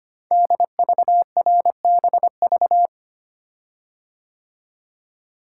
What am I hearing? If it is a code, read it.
Morse code, heard as a keyed tone, about 25 wpm, 700 Hz. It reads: DVRBV
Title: Code Groups - 5 Letters - 25wpm